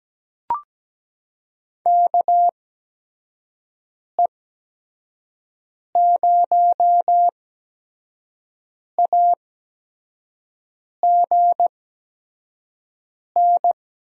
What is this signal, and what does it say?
Morse code, heard as a keyed tone, about 17 wpm, spaced as for 5 wpm, 700 Hz. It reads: KE0AGN
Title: USA Call Signs - 6 Characters - 8x Character Spacing - 17wpm